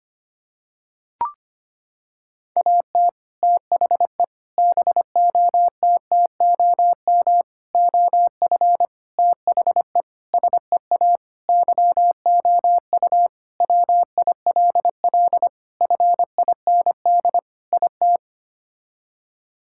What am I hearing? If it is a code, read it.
Morse code, heard as a keyed tone, about 25 wpm, 700 Hz. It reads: AT THE BOTTOM OF THE SEA YOU WILL FIND IT
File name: practice / 25wpm / Sentences with 3-Word Phrases - 25wpm